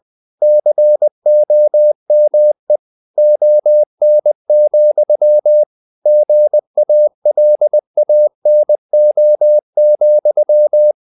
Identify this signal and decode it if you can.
Morse code, heard as a keyed tone, about 20 wpm, 600 Hz. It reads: COME ON, GALANO,
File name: the old man and the sea part 2 20wpm